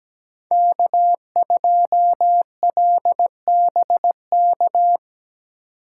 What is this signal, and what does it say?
Morse code, heard as a keyed tone, about 20 wpm, 700 Hz. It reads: K2LBK
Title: Speed Racing - USA 1x3 Call Signs 17wpm